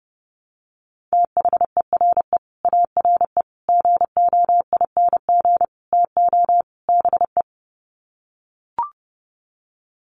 Morse code, heard as a keyed tone, about 30 wpm, 700 Hz. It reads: THERE ARE GOING TO BE
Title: Sets of 5 Words 30wpm